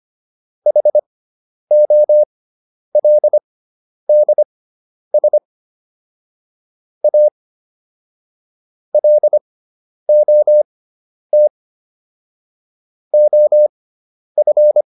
Morse code, heard as a keyed tone, about 25 wpm, spaced as for 10 wpm, 600 Hz. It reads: HOLDS A LOT OF